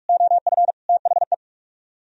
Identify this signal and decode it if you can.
Morse code, heard as a keyed tone, about 58 wpm, 700 Hz. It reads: OF THE